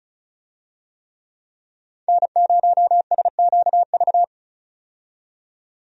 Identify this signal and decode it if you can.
Morse code, heard as a keyed tone, about 36 wpm, 700 Hz. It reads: N0SQV
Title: USA Call Signs - 5 Characters - 35wpm